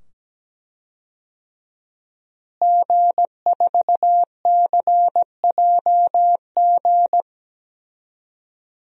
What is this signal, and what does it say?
Morse code, heard as a keyed tone, about 17 wpm, 700 Hz. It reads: G4CJG
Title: DXCC-100 Call Signs - 5 Characters - Repeated 2x - 17wpm